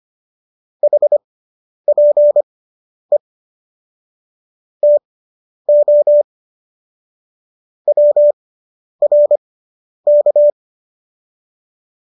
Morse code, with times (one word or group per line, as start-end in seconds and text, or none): 0.8-3.2 HPE
4.8-6.2 TO
7.9-10.5 WRK